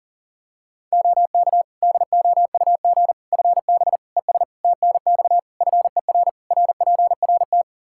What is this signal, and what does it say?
Morse code, heard as a keyed tone, about 37 wpm, 700 Hz. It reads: OK DOUG FB ES TNX FER RPRT
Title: K1USN SST QSO Practice 40wpm